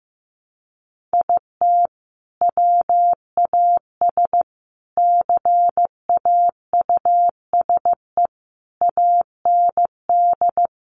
Morse code, heard as a keyed tone, about 15 wpm, 700 Hz. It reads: IT WAS CAUSE AND
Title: Sentences from Top 1000 Words 15wpm